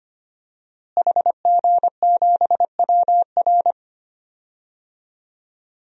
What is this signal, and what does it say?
Morse code, heard as a keyed tone, about 25 wpm, 700 Hz. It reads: HG7WR
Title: DXCC-100 Call Signs - 5 Characters - Repeated 2x - 25wpm